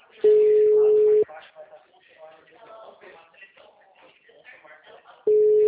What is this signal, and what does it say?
Ringback tone (Europe, 425 Hz), with speech.